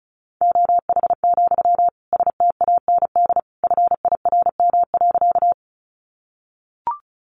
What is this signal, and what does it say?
Morse code, heard as a keyed tone, about 34 wpm, 700 Hz. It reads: OH, STAND FIRM.
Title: Sentences from Top 1400 Words 35wpm